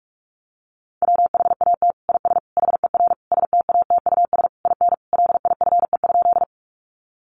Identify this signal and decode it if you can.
Morse code, heard as a keyed tone, about 44 wpm, 700 Hz. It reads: WHAT IS HER STATUS IN LIFE?
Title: Sentences from Top 1400 Words 45wpm